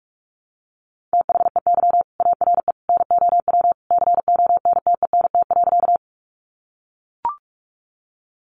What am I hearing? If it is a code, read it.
Morse code, heard as a keyed tone, about 44 wpm, 700 Hz. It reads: THEY ARE NOW CONTENT.
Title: Sentences from Top 900 Words 45wpm